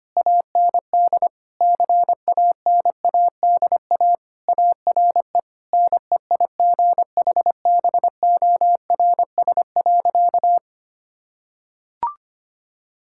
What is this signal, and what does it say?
Morse code, heard as a keyed tone, about 25 wpm, 700 Hz. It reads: AND CANADA ARE NEIGHBORS.
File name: practice / 25wpm / Sentences with 3-Word Phrases - 25wpm